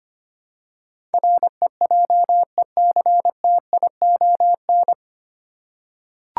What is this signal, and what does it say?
Morse code, heard as a keyed tone, about 25 wpm, 690 Hz. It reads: REJECTION
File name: practice / 25wpm / 9 Letter Words - First 3 Letters Given - 25wpm